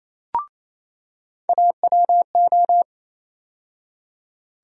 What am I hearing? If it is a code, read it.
Morse code, heard as a keyed tone, about 28 wpm, 700 Hz. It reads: AWO
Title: USA Call Sign Postfixes - 3 Letters 28wpm